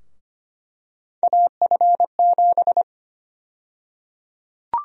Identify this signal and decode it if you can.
Morse code, heard as a keyed tone, about 25 wpm, 700 Hz. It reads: AF7
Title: USA Call Sign Prefixes - Repeated 2x - 25wpm